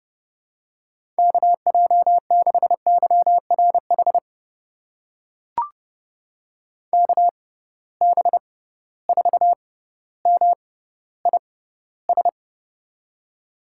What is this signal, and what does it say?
Morse code, heard as a keyed tone, about 31 wpm, spaced as for 16 wpm, 700 Hz. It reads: KJ6YRH KB4MIS
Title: USA Call Signs - 6 Characters - 6x Character Spacing - 30wpm